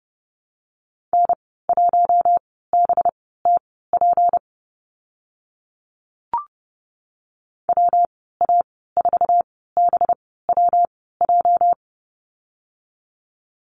Morse code, heard as a keyed tone, about 30 wpm, spaced as for 17 wpm, 700 Hz. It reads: N1BTP WA4BWJ
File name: Lesson 37g - Letter x - Call Signs 30wpm